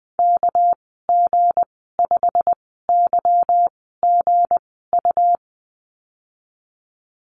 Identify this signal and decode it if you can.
Morse code, heard as a keyed tone, about 20 wpm, spaced as for 14 wpm, 700 Hz. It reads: KG5YGU
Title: 213 Lesson 34g - Number 8 - Call Signs 20wpm